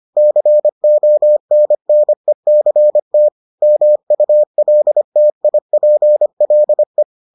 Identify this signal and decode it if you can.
Morse code, heard as a keyed tone, about 25 wpm, 600 Hz. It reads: CONNECT MULTIPLE